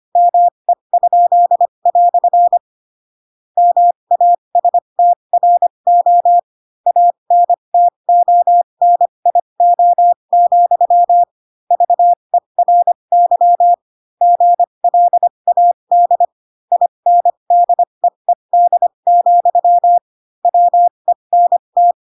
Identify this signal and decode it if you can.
Morse code, heard as a keyed tone, about 25 wpm, spaced as for 22 wpm, 700 Hz. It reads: ME?" MASTRO ANTONIO, VERY GLAD INDEED, WENT
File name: Pinocchio / Pinocchio ch2 18 25 ik1hge